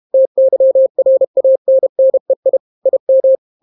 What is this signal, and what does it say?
Morse code, heard as a keyed tone, about 32 wpm, 530 Hz. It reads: TYRANNEI IM